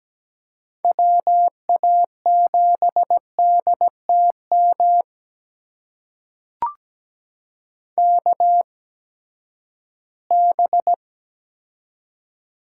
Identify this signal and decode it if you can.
Morse code, heard as a keyed tone, about 17 wpm, spaced as for 12 wpm, 700 Hz. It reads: WA7DTM KB
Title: USA Call Signs - 6 Characters - 8x Character Spacing - 17wpm